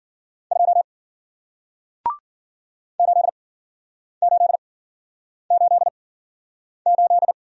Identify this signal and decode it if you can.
Morse code, heard as a keyed tone, about 53 wpm, spaced as for 9 wpm, 700 Hz. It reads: 2 8888